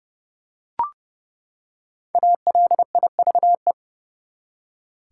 Morse code, heard as a keyed tone, about 31 wpm, 700 Hz. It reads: ALIVE
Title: Five Letter Words - 30wpm